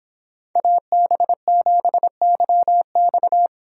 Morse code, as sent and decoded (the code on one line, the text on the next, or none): .- -... --... -.-- -..-
AB7YX